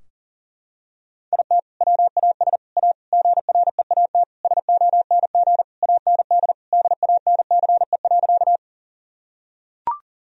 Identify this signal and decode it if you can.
Morse code, heard as a keyed tone, about 40 wpm, 700 Hz. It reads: IT WAS A GREAT SONG AND DANCE.
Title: Sentences from Top 1000 Words - Review Part 4 - 40wpm